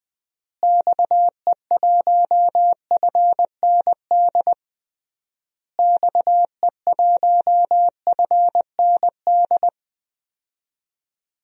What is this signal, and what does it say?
Morse code, heard as a keyed tone, about 20 wpm, 700 Hz. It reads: XE1FND XE1FND
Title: DXCC-100 Call Signs - 6 Characters - Repeated 3x - 20wpm